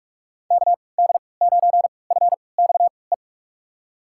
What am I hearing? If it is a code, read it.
Morse code, heard as a keyed tone, about 43 wpm, spaced as for 25 wpm, 700 Hz. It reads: KD9FXE
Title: USA Call Signs - 6 Characters - 3x Character Spacing - 45wpm